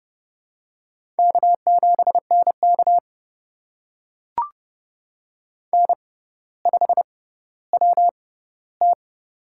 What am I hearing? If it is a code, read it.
Morse code, heard as a keyed tone, about 30 wpm, spaced as for 15 wpm, 700 Hz. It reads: K7NK N5WT
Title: USA Call Signs - 4 Characters - 6x Character Spacing - 30wpm